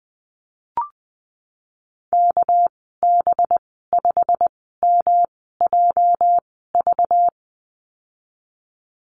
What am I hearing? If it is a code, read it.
Morse code, heard as a keyed tone, about 20 wpm, spaced as for 15 wpm, 700 Hz. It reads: KB5MJV
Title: 213 Lesson 34g - Number 8 - Call Signs 20wpm